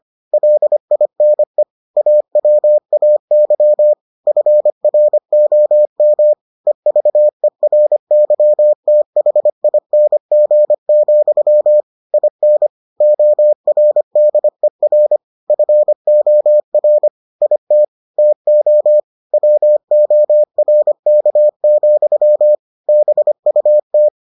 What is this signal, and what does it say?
Morse code, heard as a keyed tone, about 25 wpm, 600 Hz. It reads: LINE AWAY FROM EVERYTHING, IN ORDER FOR IT TO WORK, BUT